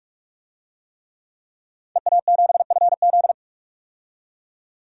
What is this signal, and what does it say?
Morse code, heard as a keyed tone, about 44 wpm, 700 Hz. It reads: EA7FZ
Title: DXCC-100 Call Signs - 5 Characters - 45wpm